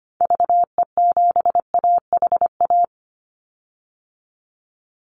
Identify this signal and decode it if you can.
Morse code, heard as a keyed tone, about 25 wpm, 700 Hz. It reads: VE7AHA
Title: Call Signs Repeated 2x - 25wpm